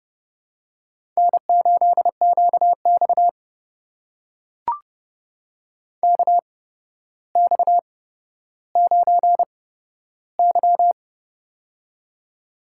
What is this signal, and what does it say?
Morse code, heard as a keyed tone, about 30 wpm, spaced as for 13 wpm, 700 Hz. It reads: N8QX KX9Y
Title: USA Call Signs - 4 Characters - 8x Character Spacing - 30wpm